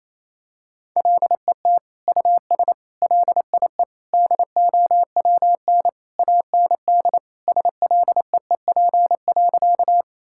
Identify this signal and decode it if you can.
Morse code, heard as a keyed tone, about 28 wpm, 700 Hz. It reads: LET US LIE DOWN AND SLEEP.
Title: Sentences from Top 1600 Words 28wpm